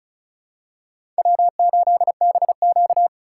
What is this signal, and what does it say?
Morse code, heard as a keyed tone, about 34 wpm, 700 Hz. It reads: W8BQ
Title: USA Call Signs - 3x Character Spacing - 35wpm